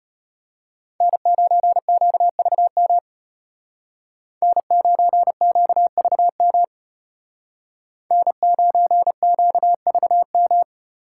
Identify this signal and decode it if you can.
Morse code, heard as a keyed tone, about 33 wpm, 700 Hz. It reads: N9QVM N9QVM N9QVM